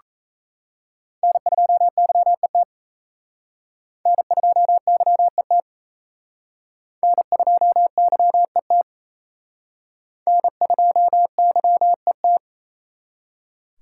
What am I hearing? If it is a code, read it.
Morse code, heard as a keyed tone, about 34 wpm, 700 Hz. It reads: N2YET N2YET N2YET N2YET
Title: Speed Racing - USA Call Signs 28wpm